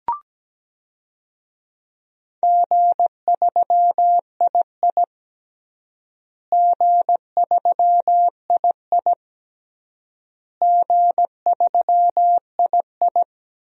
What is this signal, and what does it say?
Morse code, heard as a keyed tone, about 17 wpm, 700 Hz. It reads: G3II G3II G3II
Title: DXCC-100 Call Signs - 4 Characters - Repeated 3x - 17wpm